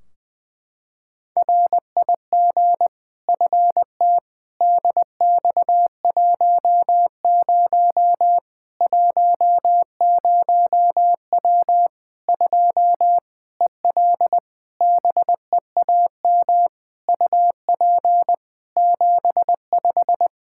Morse code, heard as a keyed tone, about 20 wpm, 700 Hz. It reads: RIG FT DX10 10W 2 EL BEAM UP 75
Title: K1USN SST QSO Practice 20wpm